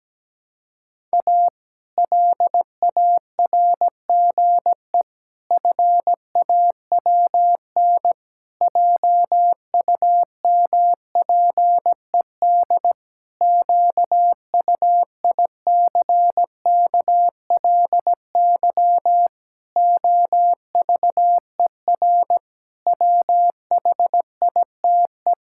Morse code, heard as a keyed tone, about 17 wpm, 700 Hz. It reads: A LARGE FAWN JUMPED QUICKLY OVER WHITE
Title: Pangram Challenge v2 - 17wpm